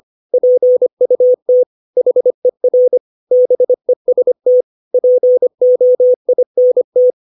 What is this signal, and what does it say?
Morse code, heard as a keyed tone, about 25 wpm, 500 Hz. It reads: PUT HER BEST POINT
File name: 632 words / 632 Words 25WPM